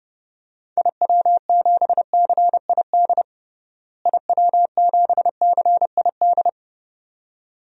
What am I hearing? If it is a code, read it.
Morse code, heard as a keyed tone, about 30 wpm, 700 Hz. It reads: IW7CID IW7CID
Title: DXCC-100 Call Signs - 6 Characters - Repeated 2x - 30wpm